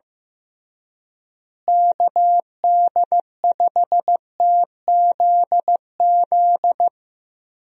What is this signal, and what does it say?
Morse code, heard as a keyed tone, about 15 wpm, 700 Hz. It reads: KD5TZZ
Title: USA Call Signs - 6 Characters - Repeated 2x - 15wpm